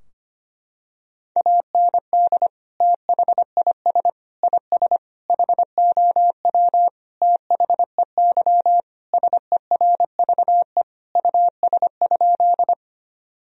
Morse code, heard as a keyed tone, about 25 wpm, 700 Hz. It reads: AND THIS IS HOW THEY SERVE US?
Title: Sentences from Top 1000 Words - Review Part 2 - 25wpm